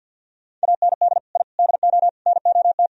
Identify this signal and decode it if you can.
Morse code, heard as a keyed tone, about 48 wpm, 700 Hz. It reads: AND I DO NOT
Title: Speed-Racing - Sets of 5 Words 50wpm